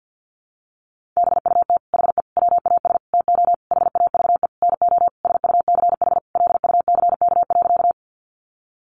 Morse code, heard as a keyed tone, about 50 wpm, 700 Hz. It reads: BUT HE WAS TO HAVE NO SUCH LUCK.
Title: Sentences from Top 1500 Words 50wpm